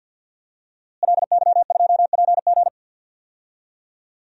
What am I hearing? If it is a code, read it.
Morse code, heard as a keyed tone, about 47 wpm, 700 Hz. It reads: LY2PG
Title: DXCC-100 Call Signs - 5 Characters - Repeated 2x - 50wpm